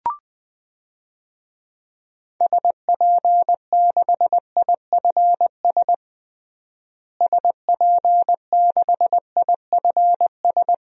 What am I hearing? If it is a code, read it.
Morse code, heard as a keyed tone, about 20 wpm, 700 Hz. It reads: SP6IFS SP6IFS
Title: DXCC-100 Call Signs - 6 Characters - Repeated 2x - 20wpm